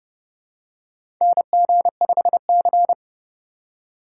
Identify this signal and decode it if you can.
Morse code, heard as a keyed tone, about 30 wpm, 700 Hz. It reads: NG5C
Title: USA Call Signs - 4 Characters - 3x Character Spacing - 30wpm